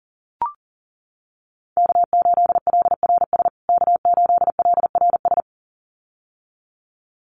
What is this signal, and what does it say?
Morse code, heard as a keyed tone, about 40 wpm, 700 Hz. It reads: K8LRS K8LRS
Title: USA Call Signs Repeated 2x 40wpm